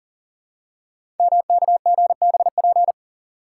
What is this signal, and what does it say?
Morse code, heard as a keyed tone, about 43 wpm, 700 Hz. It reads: MKGBP
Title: Code Groups - 5 Letters - 40wpm